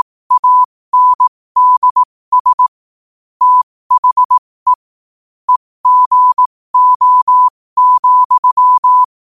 Morse code, heard as a keyed tone, about 18 wpm, spaced as for 15 wpm, 1000 Hz. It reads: ANDS THE EGO,
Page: New quotes every day in morse code at 15 Words per minute.